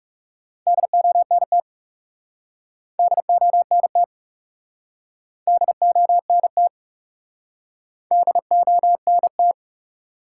Morse code, heard as a keyed tone, about 40 wpm, 700 Hz. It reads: DONT DONT DONT DONT